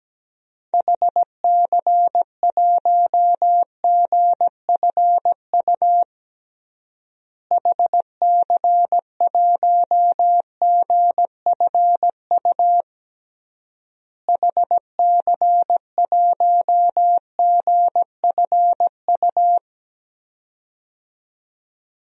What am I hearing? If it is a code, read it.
Morse code, heard as a keyed tone, about 17 wpm, 700 Hz. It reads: HC1GFU HC1GFU HC1GFU
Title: DXCC-100 Call Signs - 6 Characters - Repeated 3x - 17wpm